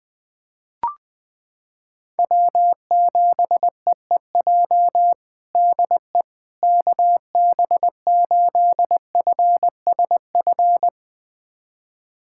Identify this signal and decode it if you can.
Morse code, heard as a keyed tone, about 20 wpm, 700 Hz. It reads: W7EEJ DE KB8FSF
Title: USA Callsign DE Callsign 20wpm